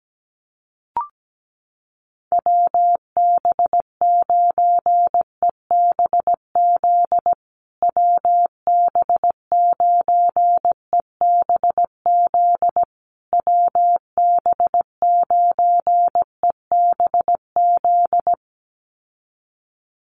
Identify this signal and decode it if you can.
Morse code, heard as a keyed tone, about 17 wpm, 700 Hz. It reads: WB9EBZ WB9EBZ WB9EBZ
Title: USA Call Signs Repeated 3x 17wpm